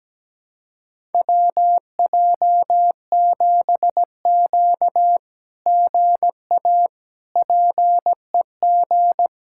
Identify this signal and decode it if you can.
Morse code, heard as a keyed tone, about 17 wpm, 700 Hz. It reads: WJ7Q GA PEG
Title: SOTA QSO Practice 17wpm